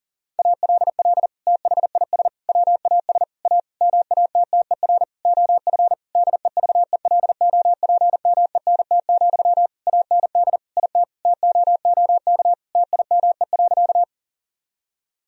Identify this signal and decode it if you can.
Morse code, heard as a keyed tone, about 40 wpm, 700 Hz. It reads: ALL THIS WAS A MATTER OF DEVELOPMENT, AND IT TOOK TIME.
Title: Sentences from Top 1000 Words - Review Part 4 - 40wpm